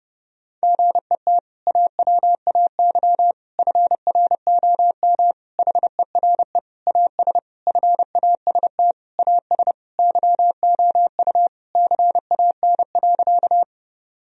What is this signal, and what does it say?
Morse code, heard as a keyed tone, about 30 wpm, 700 Hz. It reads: GET AWAY FROM HERE AS FAST AS YOU CAN.
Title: Sentences from Top 1000 Words - Review Part 2 - 30wpm